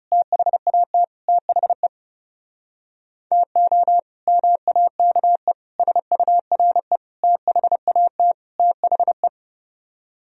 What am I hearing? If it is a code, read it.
Morse code, heard as a keyed tone, about 38 wpm, 700 Hz. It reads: THAT THE TO MAKE SURE THAT THE